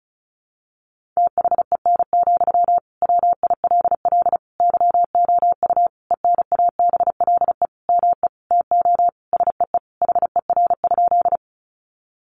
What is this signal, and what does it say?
Morse code, heard as a keyed tone, about 34 wpm, 700 Hz. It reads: THEN, WILL YOU ENABLE ME TO SEE HER?
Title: Sentences from Top 1500 Words 35wpm